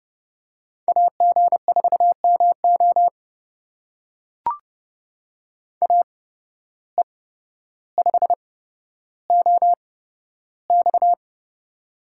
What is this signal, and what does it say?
Morse code, heard as a keyed tone, about 30 wpm, spaced as for 13 wpm, 700 Hz. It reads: AG4MO AE5OX
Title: USA Call Signs - 8x Character Spacing - 30wpm